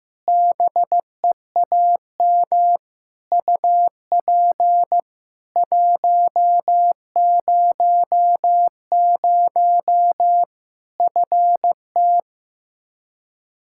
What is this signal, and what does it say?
Morse code, heard as a keyed tone, about 15 wpm, 700 Hz. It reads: BEAM UP 100 FT
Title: Practice QSO - Exchange 2 - Info - 15wpm